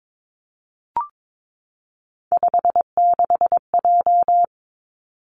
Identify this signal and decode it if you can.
Morse code, heard as a keyed tone, about 22 wpm, 700 Hz. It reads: 56J
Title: Three Challenging Letter-Number Combinations 22wpm